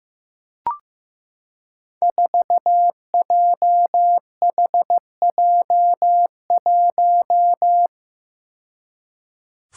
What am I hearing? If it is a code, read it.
Morse code, heard as a keyed tone, about 15 wpm, 700 Hz. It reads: 4JHJ1
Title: Five Challenging Letter-Number Combinations 15wpm